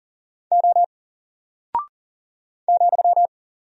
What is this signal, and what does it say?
Morse code, heard as a keyed tone, about 41 wpm, 700 Hz. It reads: O ,
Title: Single Character (Letter:Number:Punctuation:Prosign) Alt 40wpm